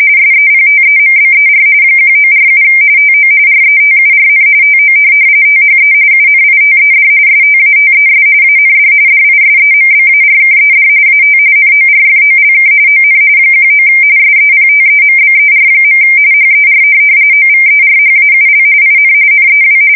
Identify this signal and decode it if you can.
RTTY, about 45.45 baud. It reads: SEPTEMBER 1947 TO RETURN TO THE UNITED STATES AND DISEMBARKED AT NORFOLK ON 19 SEPTEMBER. HER OVERHAUL IN NEW
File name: fsk170 / USS-Missouri